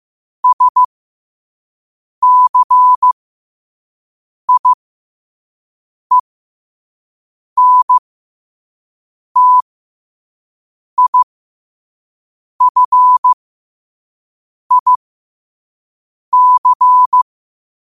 Morse code, heard as a keyed tone, about 15 wpm, spaced as for 5 wpm, 1000 Hz. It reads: SCIENTIFIC
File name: qotdfiles / 5wpm